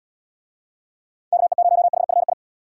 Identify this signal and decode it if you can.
Morse code, heard as a keyed tone, about 75 wpm, 700 Hz. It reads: KE0HFI